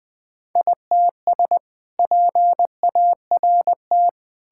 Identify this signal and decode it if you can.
Morse code, heard as a keyed tone, about 20 wpm, 700 Hz. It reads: ITS PART
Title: Sentences with 2-Word Phrases - 20wpm